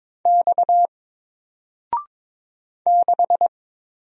Morse code, heard as a keyed tone, about 22 wpm, 700 Hz. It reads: X 6
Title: Single Letter-Number 22wpm